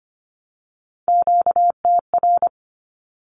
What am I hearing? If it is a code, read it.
Morse code, heard as a keyed tone, about 25 wpm, 700 Hz. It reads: QTR
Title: Q-Codes Repeated 3x 25wpm